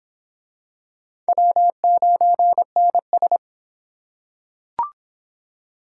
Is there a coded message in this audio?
Yes: Morse code from 1.3 to 3.4 s, heard as a keyed tone, about 22 wpm, 700 Hz. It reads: W9NS